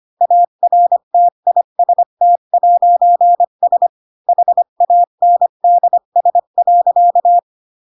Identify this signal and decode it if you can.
Morse code, heard as a keyed tone, about 25 wpm, 700 Hz. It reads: ARTIST'S HANDS.
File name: Pinocchio / Pinocchio ch8 20 25 ik1hge